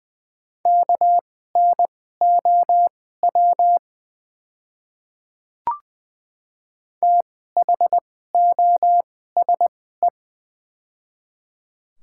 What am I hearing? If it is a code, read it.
Morse code, heard as a keyed tone, about 20 wpm, spaced as for 15 wpm, 700 Hz. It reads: KNOW THOSE
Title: 2x Character Spacing - Top 500 Words - 20wpm